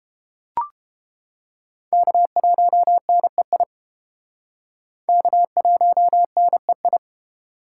Morse code, heard as a keyed tone, about 30 wpm, 700 Hz. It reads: K1NEI K1NEI